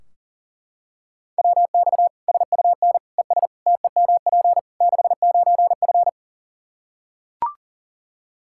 Morse code, heard as a keyed tone, about 37 wpm, 700 Hz. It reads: WX SUN ES TEMP 69F
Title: K1USN SST QSO Practice 40wpm